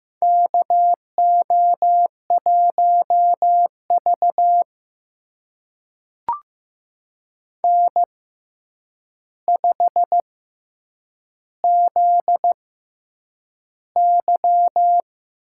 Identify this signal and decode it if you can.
Morse code, heard as a keyed tone, about 15 wpm, spaced as for 8 wpm, 700 Hz. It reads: KO1V N5ZY
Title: USA Call Signs - 4 Characters - 6x Character Spacing - 15wpm